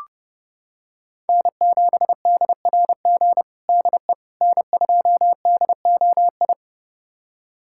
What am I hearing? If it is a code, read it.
Morse code, heard as a keyed tone, about 30 wpm, 700 Hz. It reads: N7DRG DE N2DOI
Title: USA Callsign DE Callsign - 5 Characters - 30wpm